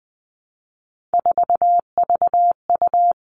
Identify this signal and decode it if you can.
Morse code, heard as a keyed tone, about 20 wpm, 700 Hz. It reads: 4VU
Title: All Cut Numbers - 3-Digit 20wpm